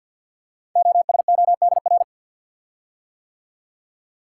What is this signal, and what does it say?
Morse code, heard as a keyed tone, about 48 wpm, 690 Hz. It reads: OSODR
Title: Code Groups - 5 Letters - 50wpm